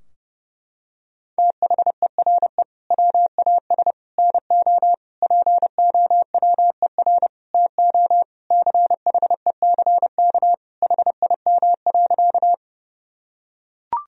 Morse code, heard as a keyed tone, about 30 wpm, 700 Hz. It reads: THERE WAS NO POWER TO CHECK HIM.
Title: Sentences from Top 1000 Words - Review Part 3 - 30wpm